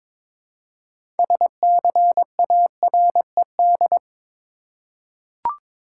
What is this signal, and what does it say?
Morse code, heard as a keyed tone, about 22 wpm, 700 Hz. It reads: SCARED
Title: Six Letter Words - 22wpm